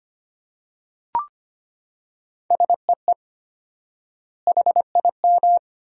Morse code, heard as a keyed tone, about 25 wpm, 700 Hz. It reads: SEE HIM
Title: 4x Word-Spacing - Sentences from Top 300 Words - 25wpm